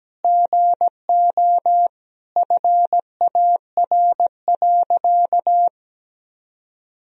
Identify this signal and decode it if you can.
Morse code, heard as a keyed tone, about 17 wpm, 700 Hz. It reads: GO FAR.
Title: Sentences from Top 400 Words and Abbreviations 17wpm